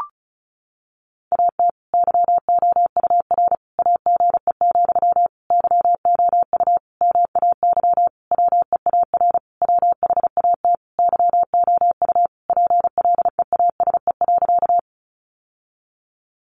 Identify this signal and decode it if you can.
Morse code, heard as a keyed tone, about 34 wpm, 700 Hz. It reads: AT YOUR AGE, YOU MAY WEAR WHAT YOU PLEASE.
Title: Sentences from Top 1500 Words 35wpm